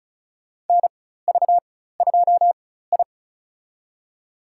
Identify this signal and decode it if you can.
Morse code, heard as a keyed tone, about 34 wpm, spaced as for 16 wpm, 700 Hz. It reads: NV2I
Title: USA Call Signs - 4x Character Spacing - 35wpm